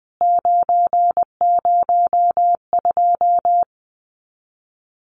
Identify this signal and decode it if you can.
Morse code, heard as a keyed tone, about 20 wpm, 700 Hz. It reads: 902
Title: CWT Exchange 20wpm